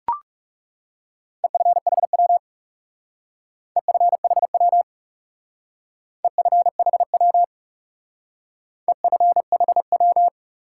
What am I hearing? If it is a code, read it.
Morse code, heard as a keyed tone, about 35 wpm, 700 Hz. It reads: EFHW EFHW EFHW EFHW
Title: Speed-Racing Top 180 QSO Elements 30wpm